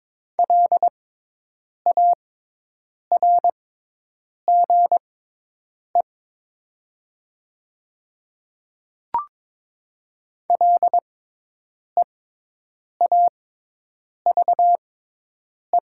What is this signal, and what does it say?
Morse code, heard as a keyed tone, about 22 wpm, spaced as for 8 wpm, 700 Hz. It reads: LARGE LEAVE
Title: 6x Character Spacing - Top 500 Words - 22wpm